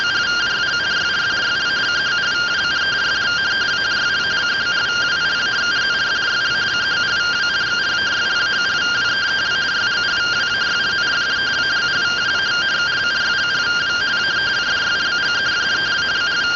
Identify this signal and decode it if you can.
Two-tone teleprinter signal, about 170 Hz shift.